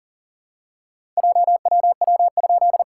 Morse code, heard as a keyed tone, about 40 wpm, 690 Hz. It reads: JWW?
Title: SOTA QSO Practice 40wpm